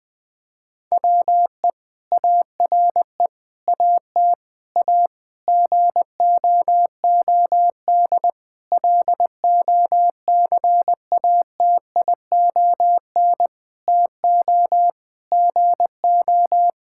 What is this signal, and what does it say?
Morse code, heard as a keyed tone, about 20 wpm, 700 Hz. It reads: WE ARE AT A GOOD LOCATION TO GO
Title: Sentences from Top 1000 Words - Review Part 6 - 20wpm